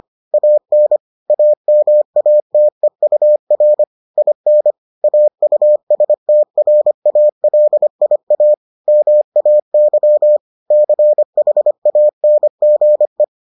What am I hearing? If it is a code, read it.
Morse code, heard as a keyed tone, about 25 wpm, 600 Hz. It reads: AN AMATEUR IN AUSTRALIA MAY CHANGE